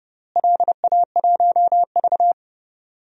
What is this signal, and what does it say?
Morse code, heard as a keyed tone, about 29 wpm, 700 Hz. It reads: LA1V
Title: DXCC-100 Call Signs - 4 Characters - Repeated 3x - 30wpm